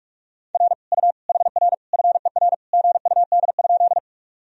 Morse code, heard as a keyed tone, about 46 wpm, 700 Hz. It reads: R U HR FER GUD?
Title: Sentences from Top 200 Words and Abbreviations 45wpm